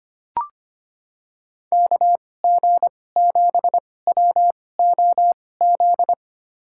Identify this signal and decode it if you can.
Morse code, heard as a keyed tone, about 25 wpm, spaced as for 18 wpm, 700 Hz. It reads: KG7WOZ
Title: USA Call Signs - 6 Characters - 2x Character Spacing - 25wpm